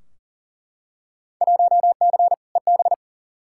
Morse code, heard as a keyed tone, about 37 wpm, 700 Hz. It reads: 1C EB
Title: Field Day Exchanges Encore - 40wpm